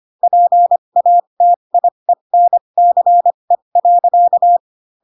Morse code, heard as a keyed tone, about 25 wpm, spaced as for 22 wpm, 700 Hz. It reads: PATIENCE.
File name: Pinocchio / Pinocchio ch4 18 25 ik1hge